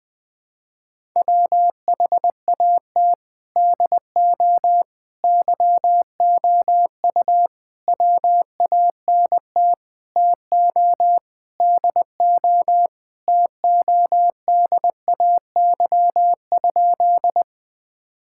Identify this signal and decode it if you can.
Morse code, heard as a keyed tone, about 21 wpm, 700 Hz. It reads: WHAT DO YOU WANT TO DO TODAY?
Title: Sentences with 4-Word Phrases - Spoken Beginning - 20wpm